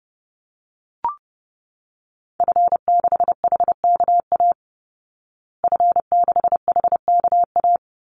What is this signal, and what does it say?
Morse code, heard as a keyed tone, about 30 wpm, 700 Hz. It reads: F6HKA F6HKA
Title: Call Signs Repeated 2x - 30wpm